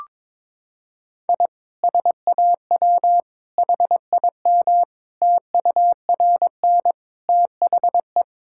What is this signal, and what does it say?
Morse code, heard as a keyed tone, about 22 wpm, 700 Hz. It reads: I SAW HIM TURN THE
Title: Sentences from Top 1600 Words 22wpm